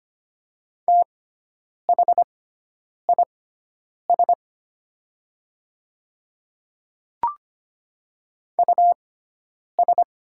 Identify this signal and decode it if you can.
Morse code, heard as a keyed tone, about 25 wpm, spaced as for 9 wpm, 700 Hz. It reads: THIS US
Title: 6x Character Spacing - Top 200 Words - 25wpm